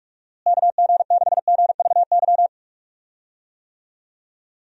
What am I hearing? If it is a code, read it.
Morse code, heard as a keyed tone, about 46 wpm, 700 Hz. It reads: KG6GVY
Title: USA Call Signs - 6 Characters - Repeated 3x - 45wpm